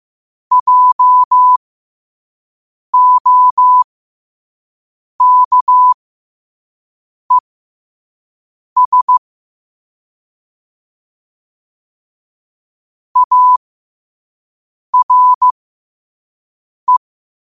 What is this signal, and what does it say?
Morse code, heard as a keyed tone, about 15 wpm, spaced as for 5 wpm, 1000 Hz. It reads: JOKES ARE
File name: qotdfiles / 5wpm